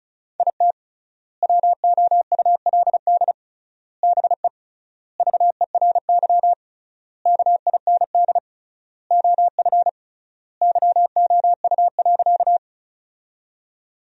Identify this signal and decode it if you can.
Morse code, heard as a keyed tone, about 34 wpm, 700 Hz. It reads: IT WOULD BE VERY KIND OF YOU.
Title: 3x Word-Spacing - Sentences from Top 200 Words - 35wpm